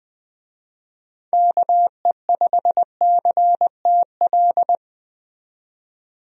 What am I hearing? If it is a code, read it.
Morse code, heard as a keyed tone, about 20 wpm, 700 Hz. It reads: KE5CTL
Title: USA Call Signs - 6 Characters - 20wpm